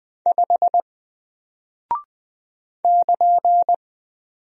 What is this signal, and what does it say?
Morse code, heard as a keyed tone, about 20 wpm, 700 Hz. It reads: 5 <KN>
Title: Single Character (Letter:Number:Punctuation:Prosign) 20wpm